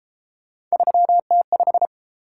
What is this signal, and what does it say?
Morse code, heard as a keyed tone, about 32 wpm, 700 Hz. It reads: 3T5